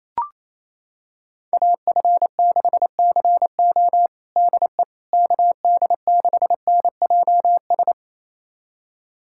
Morse code, heard as a keyed tone, about 28 wpm, 700 Hz. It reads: AF6CO DE KD6NJS
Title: USA Callsign DE Callsign 28wpm